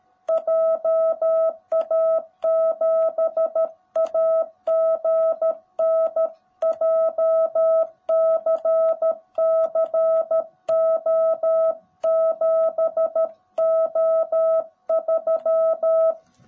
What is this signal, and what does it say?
Morse code, heard as a keyed tone, about 13 wpm, 650 Hz. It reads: JA7AGN JCCO7O3
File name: jh8cbh / 2021cw / cw18b